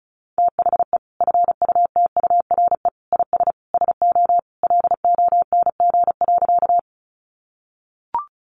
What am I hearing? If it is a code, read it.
Morse code, heard as a keyed tone, about 34 wpm, 700 Hz. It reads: THE FUTURE IS SO LONG.
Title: Sentences from Top 700 Words 35wpm